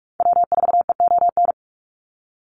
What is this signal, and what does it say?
Morse code, heard as a keyed tone, about 44 wpm, 700 Hz. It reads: W4EON
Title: NA SOTA Activator Callsigns 45wpm